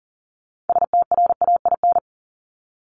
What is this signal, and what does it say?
Morse code, heard as a keyed tone, about 40 wpm, 700 Hz. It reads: STRAIN
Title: Top 1751-2000 Words 40wpm